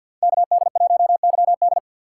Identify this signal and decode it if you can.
Morse code, heard as a keyed tone, about 47 wpm, 700 Hz. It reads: KD1YD
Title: USA Call Signs - 2x Character Spacing - 50wpm